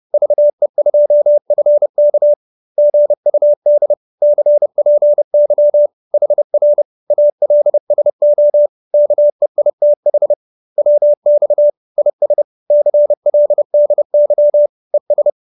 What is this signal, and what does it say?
Morse code, heard as a keyed tone, about 30 wpm, 590 Hz. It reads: VE2FK GUD CPY HR ALSO KEITH WX IS CLDY ES
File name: QSO314 30